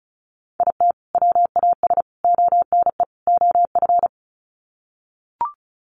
Morse code, heard as a keyed tone, about 34 wpm, 700 Hz. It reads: IT WAS ONE OF
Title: Sets of 4 Words 35wpm